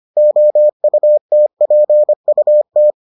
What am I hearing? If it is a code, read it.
Morse code, heard as a keyed tone, about 25 wpm, 600 Hz. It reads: OUTPUT